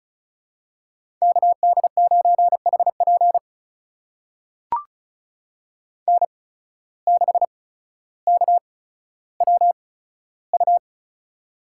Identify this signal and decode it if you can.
Morse code, heard as a keyed tone, about 36 wpm, spaced as for 15 wpm, 700 Hz. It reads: KD9HP N6KWU
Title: USA Call Signs - 5 Characters - 8x Character Spacing - 35wpm